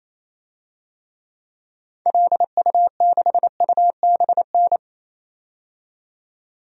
Morse code, heard as a keyed tone, about 28 wpm, 700 Hz. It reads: LU6UBN
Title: DXCC-100 Call Signs - 6 Characters - 28wpm